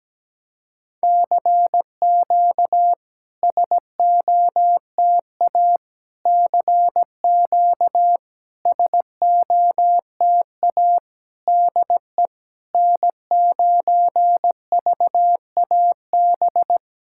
Morse code, heard as a keyed tone, about 17 wpm, 660 Hz. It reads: CQ SOTA CQ SOTA DE N9VAB
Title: SOTA QSO Practice 17wpm